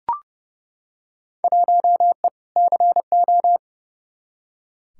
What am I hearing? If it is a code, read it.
Morse code, heard as a keyed tone, about 30 wpm, 700 Hz. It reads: 1E CO
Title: Field Day Exchanges - 30wpm